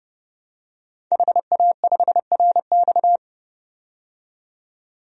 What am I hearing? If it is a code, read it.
Morse code, heard as a keyed tone, about 30 wpm, 700 Hz. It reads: HA5RX
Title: DXCC-100 Call Signs - 5 Characters - 30wpm